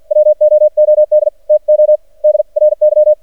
Morse code, heard as a keyed tone, about 54 wpm, spaced as for 45 wpm, 610 Hz. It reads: WOOD TO DRY
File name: archived / CRSnet